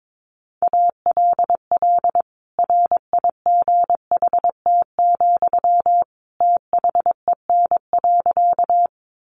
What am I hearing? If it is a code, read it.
Morse code, heard as a keyed tone, about 22 wpm, 700 Hz. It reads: ALL RIGHT, THEN.
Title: Sentences from Top 200 Words 22wpm